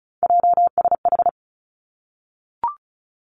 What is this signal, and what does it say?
Morse code, heard as a keyed tone, about 36 wpm, 700 Hz. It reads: JSH
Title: Three Challenging Letter-Number Combinations 35wpm